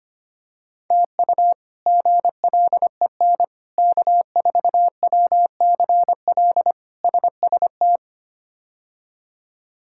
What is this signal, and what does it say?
Morse code, heard as a keyed tone, about 25 wpm, 700 Hz. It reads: TU GLEN K4WCL SST
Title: K1USN SST QSO Practice - 25wpm